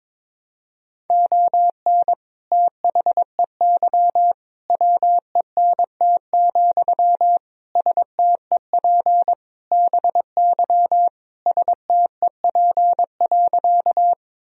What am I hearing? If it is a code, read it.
Morse code, heard as a keyed tone, about 22 wpm, 700 Hz. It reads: ON THEY WENT, STEP BY STEP.
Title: Sentences from Top 1000 Words - Review Part 2 - 22wpm